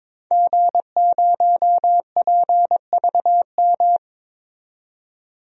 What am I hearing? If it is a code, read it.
Morse code, heard as a keyed tone, about 22 wpm, 700 Hz. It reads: G0PVM
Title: DXCC-100 Call Signs - 5 Characters - 22wpm